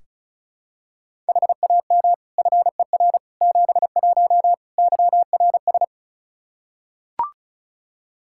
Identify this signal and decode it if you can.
Morse code, heard as a keyed tone, about 34 wpm, 700 Hz. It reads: HAM FER 71 YRS
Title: Practice QSO - Exchange 3 - More Info - 35wpm